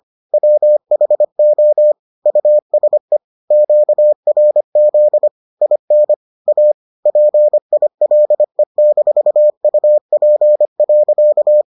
Morse code, heard as a keyed tone, about 25 wpm, 600 Hz. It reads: WHO USE QRZ IN A PILE-UP.